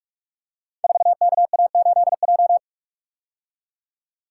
Morse code, heard as a keyed tone, about 45 wpm, 700 Hz. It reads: 4KA8J
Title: Code Groups - 5 Letter:Numbers - 45wpm